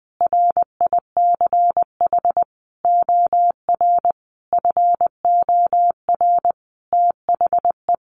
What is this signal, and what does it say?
Morse code, heard as a keyed tone, about 20 wpm, 700 Hz. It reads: RICH OR FOR THE
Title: Sentences from Top 1100 Words 20wpm